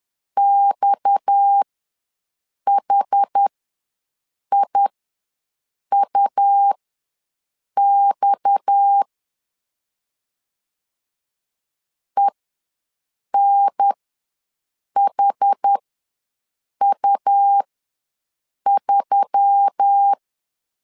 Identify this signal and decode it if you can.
Morse code, heard as a keyed tone, about 10 wpm, spaced as for 6 wpm, 800 Hz. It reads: XHIUX ENHU3
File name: LEZIONI / 6 PKX0 / 6 PKX ws 10wpm 3706